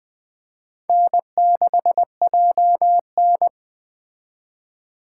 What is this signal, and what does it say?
Morse code, heard as a keyed tone, about 21 wpm, 700 Hz. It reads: N6JN